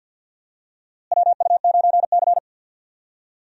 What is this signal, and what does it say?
Morse code, heard as a keyed tone, about 47 wpm, 700 Hz. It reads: WU9C
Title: USA Call Signs - 4 Characters - Repeated 2x - 50wpm